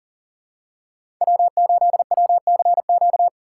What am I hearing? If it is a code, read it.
Morse code, heard as a keyed tone, about 40 wpm, 700 Hz. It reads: W8WCQ
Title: USA Call Signs - 5 Characters - 3x Character Spacing - 40wpm